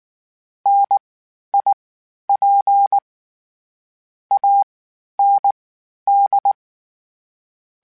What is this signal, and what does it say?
Morse code, heard as a keyed tone, about 19 wpm, spaced as for 11 wpm, 800 Hz. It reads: NIP AND
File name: Misc / Misc-mp3s / WordPairs sp @18